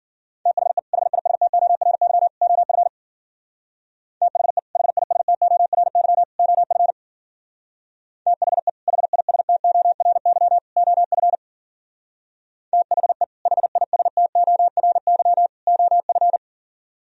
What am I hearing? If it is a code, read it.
Morse code, heard as a keyed tone, about 50 wpm, 700 Hz. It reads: THE HISTORY OF THE HISTORY OF THE HISTORY OF THE HISTORY OF
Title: Speed-Racing - Sets of 3 Words 40wpm